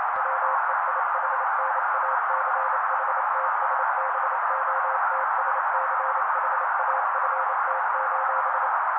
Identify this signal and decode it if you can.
Morse code, heard as a keyed tone, about 28 wpm, 530 Hz. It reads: WEIHNACHTSBOTSCHAFT:
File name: daily / 27wpm noise